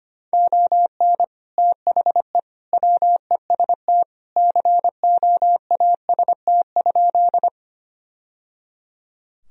Morse code, heard as a keyed tone, about 25 wpm, 700 Hz. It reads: ON THE WEST COAST?
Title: Sentences from Top 1000 Words - Review Part 4 - 25wpm